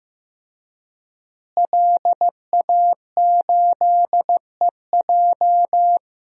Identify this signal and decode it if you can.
Morse code, heard as a keyed tone, about 15 wpm, 700 Hz. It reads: LA8EJ
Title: DXCC-100 Call Signs - 5 Characters - Repeated 2x - 15wpm